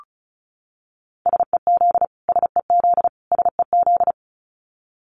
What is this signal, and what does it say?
Morse code, heard as a keyed tone, about 34 wpm, 700 Hz. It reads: SEZ SEZ SEZ
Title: Abbreviations Repeated 3x - 35wpm